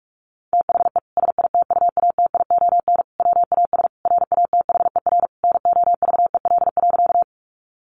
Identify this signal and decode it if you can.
Morse code, heard as a keyed tone, about 44 wpm, 700 Hz. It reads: THE SITUATION WAS RATHER NOVEL.
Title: Sentences from Top 1300 Words 45wpm